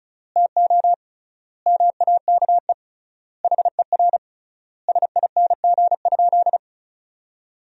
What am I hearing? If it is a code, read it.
Morse code, heard as a keyed tone, about 36 wpm, 700 Hz. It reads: TO MAKE HER SING?
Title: 3x Word-Spacing - Sentences from Top 400 Words - 35wpm